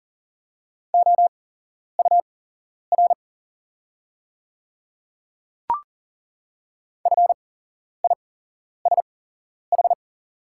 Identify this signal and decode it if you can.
Morse code, heard as a keyed tone, about 40 wpm, spaced as for 11 wpm, 700 Hz. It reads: OUR FISH
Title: 8x Character Spacing - Top 500 Words - 40wpm